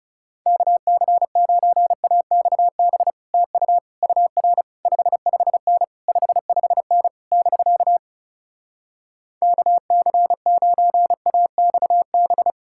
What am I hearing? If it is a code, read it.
Morse code, heard as a keyed tone, about 38 wpm, 700 Hz. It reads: KC9AXB TU UR 55N 55N <BK> KC9AXB
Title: Speed-Racing Common QSO Phrases Encore - 30wpm